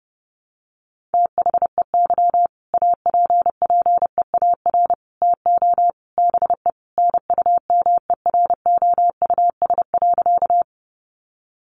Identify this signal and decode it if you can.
Morse code, heard as a keyed tone, about 30 wpm, 700 Hz. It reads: THEY APPEAR TO BE NUMEROUS.
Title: Sentences from Top 1400 Words 30wpm